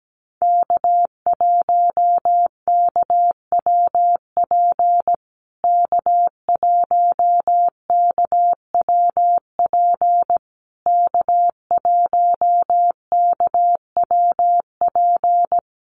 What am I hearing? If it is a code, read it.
Morse code, heard as a keyed tone, about 17 wpm, 700 Hz. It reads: K1KWP K1KWP K1KWP
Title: USA Call Signs Repeated 3x 17wpm